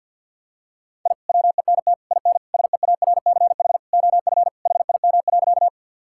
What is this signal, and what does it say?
Morse code, heard as a keyed tone, about 52 wpm, 700 Hz. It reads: I WENT IN SEARCH OF HIM.
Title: Sentences from Top 1000 Words - Review Part 5 - 50wpm